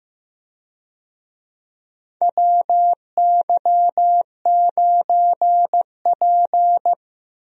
Morse code, heard as a keyed tone, about 15 wpm, 700 Hz. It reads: WY9P
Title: USA Call Signs - 4 Characters - 15wpm